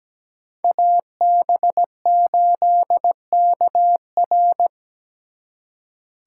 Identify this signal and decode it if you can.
Morse code, heard as a keyed tone, about 20 wpm, 700 Hz. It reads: AB8KR
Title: Speed Racing - USA 2x2 Call Signs 17wpm